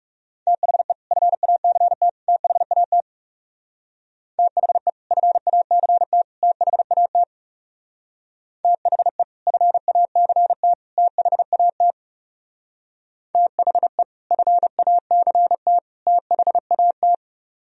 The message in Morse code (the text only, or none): THE FACT THAT THE FACT THAT THE FACT THAT THE FACT THAT